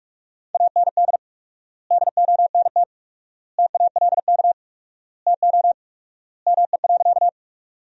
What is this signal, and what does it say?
Morse code, heard as a keyed tone, about 47 wpm, 700 Hz. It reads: AND DONT TALK TO ME.
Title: 4x Word-Spacing - Sentences from Top 400 Words - 45wpm